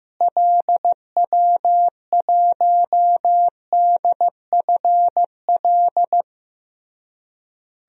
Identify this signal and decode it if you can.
Morse code, heard as a keyed tone, about 15 wpm, 700 Hz. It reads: LW1DFL
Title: DXCC-100 Call Signs - 6 Characters - Repeated 3x - 15wpm